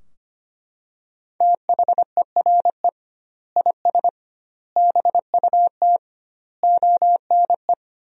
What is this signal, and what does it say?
Morse code, heard as a keyed tone, about 25 wpm, 700 Hz. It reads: THERE IS BUT ONE
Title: 2x Word-Spacing - Sentences from Top 100 Words - 25wpm